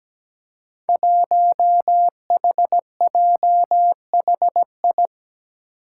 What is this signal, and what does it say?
Morse code, heard as a keyed tone, about 17 wpm, 700 Hz. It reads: 1HJHI
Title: Five Challenging Letter-Number Combinations 17wpm